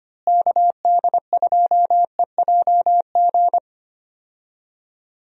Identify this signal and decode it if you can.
Morse code, heard as a keyed tone, about 25 wpm, 700 Hz. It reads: KD2EJG
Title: USA Call Signs - 6 Characters - Repeated 3x - 25wpm